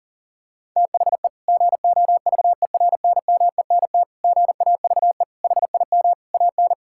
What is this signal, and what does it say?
Morse code, heard as a keyed tone, about 37 wpm, 700 Hz. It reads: THE GOVERNMENT GAVE HIM AN
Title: Sentences from Top 1000 Words - Review Part 5 - 40wpm